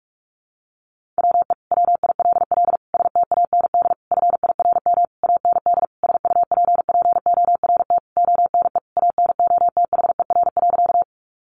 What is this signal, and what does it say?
Morse code, heard as a keyed tone, about 44 wpm, 700 Hz. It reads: WE WILL STAND FIRM AND SUPPORT ONE ANOTHER.
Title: Sentences from Top 1400 Words 45wpm